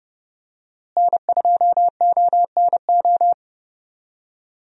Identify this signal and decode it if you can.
Morse code, heard as a keyed tone, about 31 wpm, 700 Hz. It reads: N2ONO
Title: USA Call Signs - 5 Characters - Repeated 3x - 30wpm